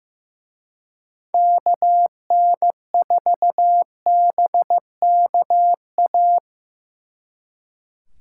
Morse code, heard as a keyed tone, about 17 wpm, 700 Hz. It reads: KN4BKA
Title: Speed Racing - USA 2x3 Call Signs 15wpm